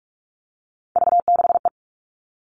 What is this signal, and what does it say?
Morse code, heard as a keyed tone, about 44 wpm, 700 Hz. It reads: V6E
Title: All Cut Numbers - 3-Digit 45wpm